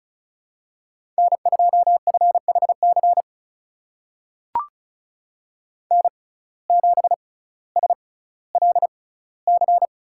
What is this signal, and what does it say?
Morse code, heard as a keyed tone, about 36 wpm, spaced as for 18 wpm, 700 Hz. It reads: N2FHC N7SLC
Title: USA Call Signs - 5 Characters - 6x Character Spacing - 35wpm